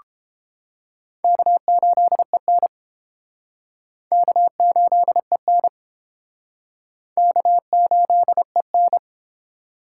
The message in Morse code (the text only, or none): K8EN K8EN K8EN